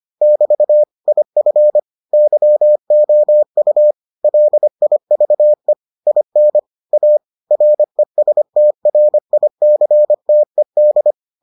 Morse code, heard as a keyed tone, about 25 wpm, 600 Hz. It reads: <BT> IF YOU LIVE IN A RESTRICTED